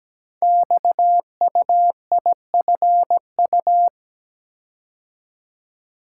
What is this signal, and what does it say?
Morse code, heard as a keyed tone, about 17 wpm, 700 Hz. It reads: XUIFU
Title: Code Groups - 5 Letters - 17wpm